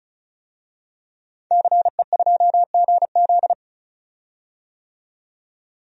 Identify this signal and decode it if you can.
Morse code, heard as a keyed tone, about 36 wpm, 700 Hz. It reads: CE2GZ
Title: DXCC-100 Call Signs - 5 Characters - 35wpm